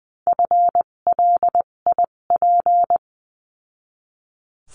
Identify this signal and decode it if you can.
Morse code, heard as a keyed tone, about 20 wpm, spaced as for 17 wpm, 700 Hz. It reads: FLIP
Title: 257 Lesson 42b - Words 14-20wpm